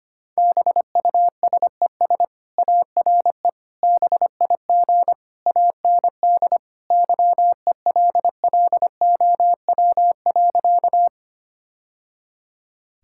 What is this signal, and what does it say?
Morse code, heard as a keyed tone, about 25 wpm, 700 Hz. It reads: BUSES ARE BIG AND YELLOW.
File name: Easy Sentences with Hints - 25wpm